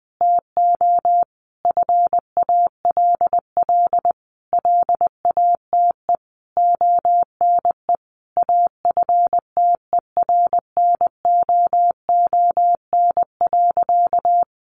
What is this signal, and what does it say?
Morse code, heard as a keyed tone, about 20 wpm, 700 Hz. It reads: TO FALL LATE ONE AFTERNOON.
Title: Sentences from Top 1500 Words 20wpm